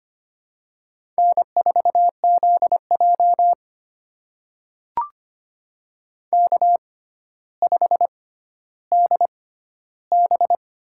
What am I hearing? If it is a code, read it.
Morse code, heard as a keyed tone, about 25 wpm, spaced as for 13 wpm, 700 Hz. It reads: N4ZJ K5DB
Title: USA Call Signs - 6x Character Spacing - 25wpm